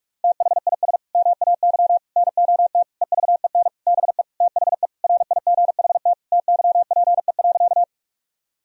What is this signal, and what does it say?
Morse code, heard as a keyed tone, about 41 wpm, 700 Hz. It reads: THIS MAY NOT EVEN BE THE RIGHT TYPE.
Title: Sentences from Top 1000 Words - Review Part 3 - 45wpm